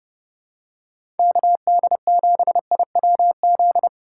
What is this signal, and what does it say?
Morse code, heard as a keyed tone, about 30 wpm, 700 Hz. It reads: KD7IWZ
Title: USA Call Signs - 6 Characters - 3x Character Spacing - 30wpm